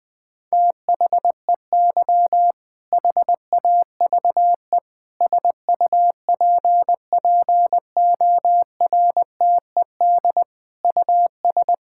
Morse code, heard as a keyed tone, about 20 wpm, 700 Hz. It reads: THEY HAVE SUPPORTED US
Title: Sentences with 6-Word Phrases - 20wpm